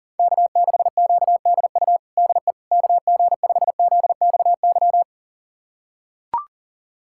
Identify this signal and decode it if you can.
Morse code, heard as a keyed tone, about 40 wpm, 700 Hz. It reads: K6QDU DE KG5ZXY
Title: USA Callsign DE Callsign 40wpm